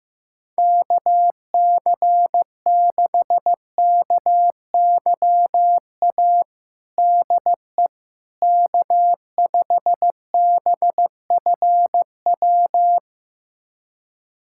Morse code, heard as a keyed tone, about 15 wpm, 700 Hz. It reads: KC6KYA DE K5BFW
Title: USA Callsign DE Callsign 15wpm